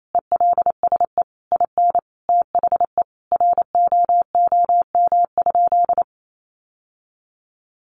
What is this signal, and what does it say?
Morse code, heard as a keyed tone, about 28 wpm, 700 Hz. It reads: ELSE IN THE ROOM?
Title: Sentences from Top 900 Words 28wpm